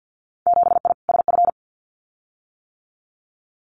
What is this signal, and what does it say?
Morse code, heard as a keyed tone, about 50 wpm, 700 Hz. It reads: 7I SF
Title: Winter Field Day Exchanges Encore 50wpm